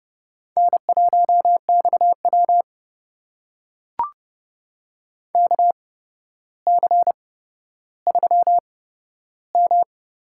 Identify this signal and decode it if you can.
Morse code, heard as a keyed tone, about 30 wpm, spaced as for 13 wpm, 700 Hz. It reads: N1XW KC3M
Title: USA Call Signs - 8x Character Spacing - 30wpm